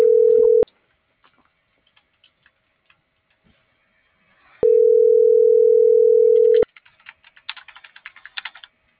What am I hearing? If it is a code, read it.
Ringback tone (North America), 440 + 480 Hz.